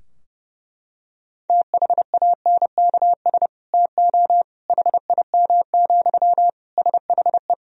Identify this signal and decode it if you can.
Morse code, heard as a keyed tone, about 30 wpm, 700 Hz. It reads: THANKS TO HIM, SHE
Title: Sentences from Top 1000 Words - Review Part 3 - 30wpm